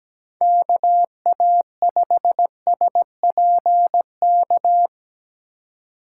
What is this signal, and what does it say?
Morse code, heard as a keyed tone, about 21 wpm, 700 Hz. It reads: KA5SPK
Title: Speed Racing - USA 2x3 Call Signs 17wpm